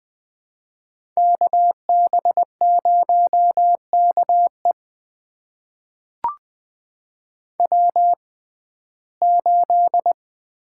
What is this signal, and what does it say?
Morse code, heard as a keyed tone, about 20 wpm, spaced as for 14 wpm, 700 Hz. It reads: KB0KE W8
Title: USA Call Signs - 5 Characters - 6x Character Spacing - 20wpm